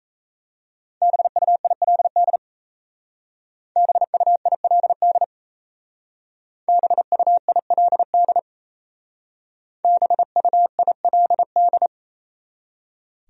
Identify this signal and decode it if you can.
Morse code, heard as a keyed tone, about 36 wpm, 700 Hz. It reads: BUILD BUILD BUILD BUILD